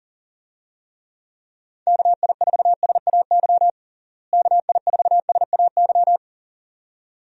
Morse code, heard as a keyed tone, about 37 wpm, 700 Hz. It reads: KI4SAY KI4SAY
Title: USA Call Signs - 6 Characters - Repeated 2x - 40wpm